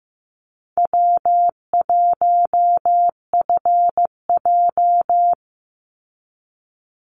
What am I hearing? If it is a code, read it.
Morse code, heard as a keyed tone, about 15 wpm, 700 Hz. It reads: W1FJ
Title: Easy Call Signs - 15wpm